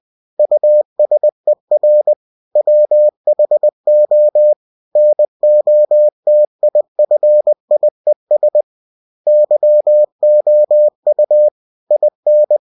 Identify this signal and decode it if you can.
Morse code, heard as a keyed tone, about 20 wpm, 600 Hz. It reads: USER WHO NOTIFIES YOU IN
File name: cw-books / winnie-the-pooh / winnie-20wpm-page-0117